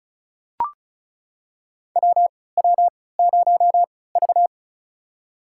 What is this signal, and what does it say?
Morse code, heard as a keyed tone, about 36 wpm, spaced as for 20 wpm, 700 Hz. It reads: WW0V
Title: USA Call Signs - 4 Characters - 3x Character Spacing - 35wpm